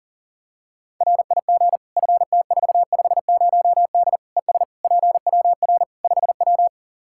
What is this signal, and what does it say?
Morse code, heard as a keyed tone, about 40 wpm, 700 Hz. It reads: RIG FT450D ES PWR 5W
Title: Practice QSO - Exchange 2 - Info - 40wpm